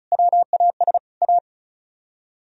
Morse code, heard as a keyed tone, about 43 wpm, 690 Hz. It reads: WAS A
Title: Speed-Racing - Sets of 3 Words 35wpm